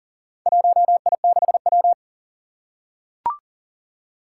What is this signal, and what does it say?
Morse code, heard as a keyed tone, about 37 wpm, 700 Hz. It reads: 1I6W
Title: Four Challenging Letter-Number Combinations 40wpm